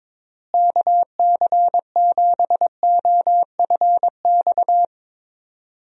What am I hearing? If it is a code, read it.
Morse code, heard as a keyed tone, about 22 wpm, 700 Hz. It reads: KC7OFX
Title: USA Call Signs - 6 Characters - Repeated 3x - 22wpm